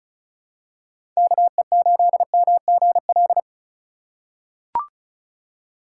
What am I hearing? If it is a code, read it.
Morse code, heard as a keyed tone, about 34 wpm, 700 Hz. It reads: KE8MGL
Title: USA Call Signs - 6 Characters - 2x Character Spacing - 35wpm